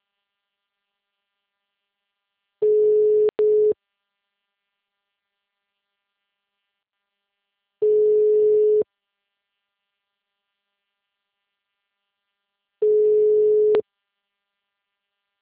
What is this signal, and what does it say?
Ringback tone (Europe, 425 Hz).
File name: out- -2028- - - .70